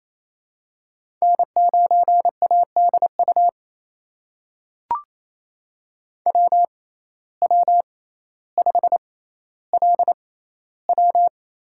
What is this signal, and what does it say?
Morse code, heard as a keyed tone, about 28 wpm, spaced as for 14 wpm, 700 Hz. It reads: N9ADU WW5LW
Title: USA Call Signs - 5 Characters - 6x Character Spacing - 28wpm